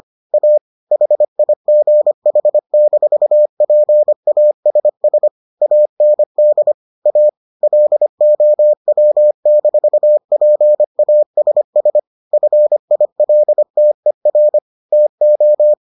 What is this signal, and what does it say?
Morse code, heard as a keyed tone, about 25 wpm, 600 Hz. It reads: A HIGH-PASS AND A LOW-PASS FILTER TO